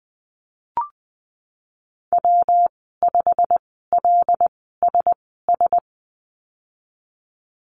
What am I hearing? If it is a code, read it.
Morse code, heard as a keyed tone, about 20 wpm, spaced as for 15 wpm, 700 Hz. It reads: W5LSS
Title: 179 Lesson 29g - Slash - Call Signs 20wpm